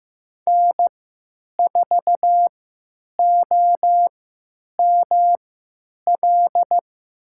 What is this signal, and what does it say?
Morse code, heard as a keyed tone, about 15 wpm, spaced as for 9 wpm, 700 Hz. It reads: N4OML
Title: USA Call Signs - 3x Character Spacing - 15wpm